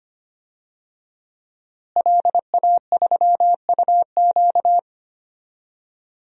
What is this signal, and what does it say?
Morse code, heard as a keyed tone, about 25 wpm, 700 Hz. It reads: LA3UQ
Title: DXCC-100 Call Signs - 5 Characters - Repeated 2x - 25wpm